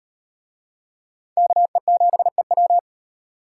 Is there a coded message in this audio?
Yes: Morse code from 1.4 to 2.8 s, heard as a keyed tone, about 38 wpm, 700 Hz. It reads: KE7EW